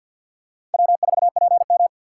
Morse code, heard as a keyed tone, about 48 wpm, 690 Hz. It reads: W4PM
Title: USA Call Signs - 4 Characters - 6x Character Spacing - 50wpm